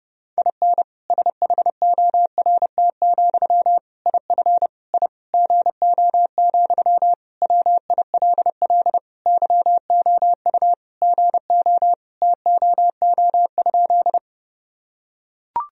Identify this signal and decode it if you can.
Morse code, heard as a keyed tone, about 30 wpm, 700 Hz. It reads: IN SHORT, IF I GO, WILL YOU GO TOO?
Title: Sentences from Top 1000 Words - Review Part 2 - 30wpm